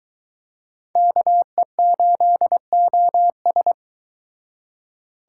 Morse code, heard as a keyed tone, about 23 wpm, 700 Hz. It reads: KE8OS